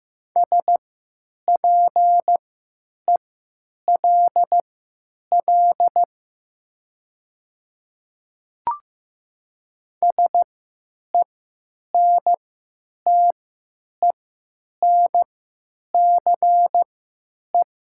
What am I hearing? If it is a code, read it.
Morse code, heard as a keyed tone, about 15 wpm, spaced as for 9 wpm, 700 Hz. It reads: SPELL SENTENCE
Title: 3x Character Spacing - Top 200 Words - 15wpm